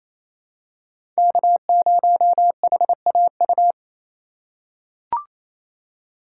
Morse code, heard as a keyed tone, about 28 wpm, 700 Hz. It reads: K0HAU
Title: USA Call Signs - 5 Characters - 4x Character Spacing - 28wpm